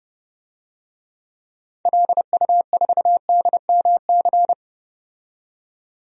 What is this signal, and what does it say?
Morse code, heard as a keyed tone, about 32 wpm, 700 Hz. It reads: LU4DMC
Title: DXCC-100 Call Signs - 6 Characters - 30wpm